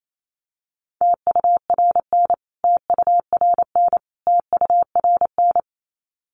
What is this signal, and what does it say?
Morse code, heard as a keyed tone, about 28 wpm, 700 Hz. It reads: TURN TURN TURN
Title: Top 200 Words Repeated 28wpm